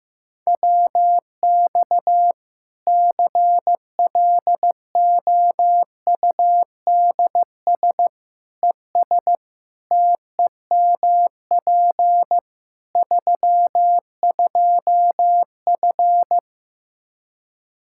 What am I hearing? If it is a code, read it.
Morse code, heard as a keyed tone, about 15 wpm, 700 Hz. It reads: WX CLOUDS ES TEMP 32F
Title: Practice QSO - Exchange 2 - Info - 15wpm